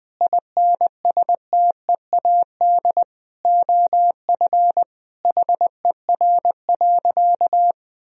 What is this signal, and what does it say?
Morse code, heard as a keyed tone, about 21 wpm, 700 Hz. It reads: INSTEAD OF HER.
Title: Sentences from Top 1000 Words - Review Part 4 - 20wpm